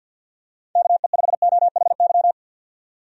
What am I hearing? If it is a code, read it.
Morse code, heard as a keyed tone, about 50 wpm, 700 Hz. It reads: KE5OHY
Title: USA Call Signs - 6 Characters - 50wpm